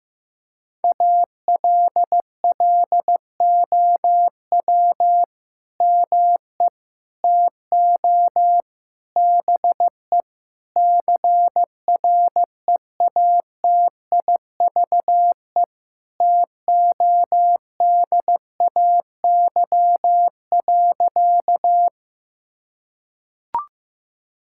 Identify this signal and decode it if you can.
Morse code, heard as a keyed tone, about 15 wpm, 700 Hz. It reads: ALLOW ME TO BE CREATIVE TODAY.
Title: Sentences from Top 1000 Words - Review Part 5 - 15wpm